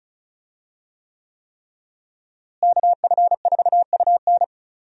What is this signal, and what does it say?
Morse code, heard as a keyed tone, about 34 wpm, 700 Hz. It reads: KF4UN
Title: USA Call Signs - 5 Characters - Repeated 3x - 35wpm